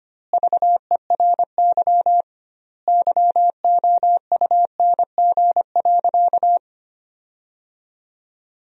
Morse code, heard as a keyed tone, about 25 wpm, 700 Hz. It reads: VERY YOUNG.
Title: 2x Word-Spacing - Sentences from Top 400 Words - 25wpm